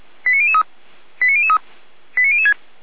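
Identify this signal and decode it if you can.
Five-tone selective call (ZVEI), heard as quick runs of five tones, tones of about 70 ms. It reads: 89003 89003 89006
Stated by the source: ZVEI five-tone selective call received by SDR.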